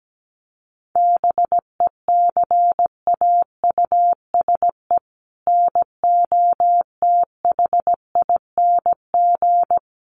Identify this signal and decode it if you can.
Morse code, heard as a keyed tone, about 17 wpm, 700 Hz. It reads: BECAUSE NOTHING
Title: Sentences from Top 1100 Words 17wpm